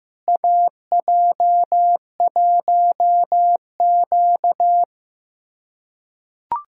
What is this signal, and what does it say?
Morse code, heard as a keyed tone, about 15 wpm, 700 Hz. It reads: AJ1Q
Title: USA Callsign DE Callsign - 4 Characters - 15wpm